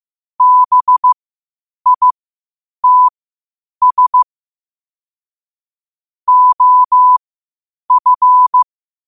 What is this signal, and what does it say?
Morse code, heard as a keyed tone, about 15 wpm, spaced as for 8 wpm, 1000 Hz. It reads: BITS OF
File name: qotdfiles / 8wpm